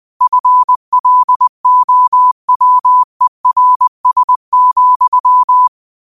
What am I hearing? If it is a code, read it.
Morse code, heard as a keyed tone, about 20 wpm, 1000 Hz. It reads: FLOWERS,
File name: qotdfiles / 20wpm